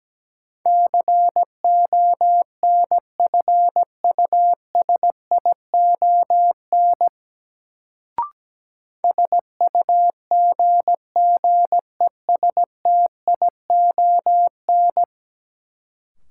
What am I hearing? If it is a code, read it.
Morse code, heard as a keyed tone, about 17 wpm, 700 Hz. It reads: CONFUSION SUGGESTION
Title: Sentences from Top 1600 Words - New Word Drill - 17wpm